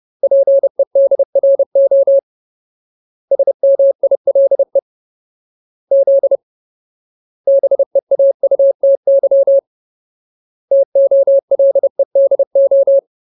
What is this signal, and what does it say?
Morse code, heard as a keyed tone, about 30 wpm, 550 Hz. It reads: PEDRO SMILE Z BEAUTY TOLEDO
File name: Ultimate Word List / 30WPM Ultimate